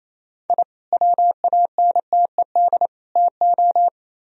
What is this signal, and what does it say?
Morse code, heard as a keyed tone, about 36 wpm, 700 Hz. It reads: I WANTED TO